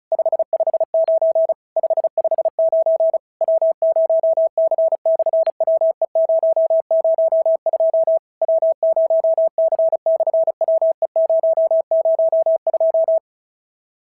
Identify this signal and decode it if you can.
Morse code, heard as a keyed tone, about 36 wpm, 650 Hz. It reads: 559 559 W0C/WE002 W0C/WE002
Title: SOTA QSO Practice 35wpm